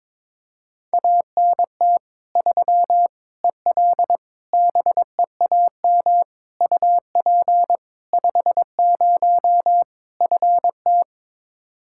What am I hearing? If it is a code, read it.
Morse code, heard as a keyed tone, about 22 wpm, 700 Hz. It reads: ANT 3 EL BEAM UP 50 FT
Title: Practice QSO - Exchange 2 - Info - 22wpm